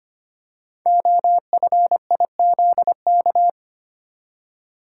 Morse code, heard as a keyed tone, about 25 wpm, 700 Hz. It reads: OFIZK
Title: Code Groups - 5 Letters - 25wpm